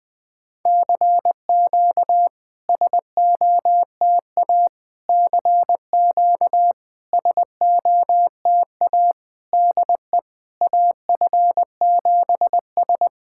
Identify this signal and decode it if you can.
Morse code, heard as a keyed tone, about 21 wpm, 700 Hz. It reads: CQ SOTA CQ SOTA DE AF7S
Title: SOTA QSO Practice 20wpm